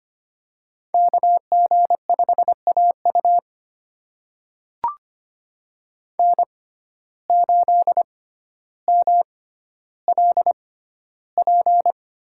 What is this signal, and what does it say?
Morse code, heard as a keyed tone, about 25 wpm, spaced as for 13 wpm, 700 Hz. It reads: KG5AU N8MLP
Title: USA Call Signs - 5 Characters - 6x Character Spacing - 25wpm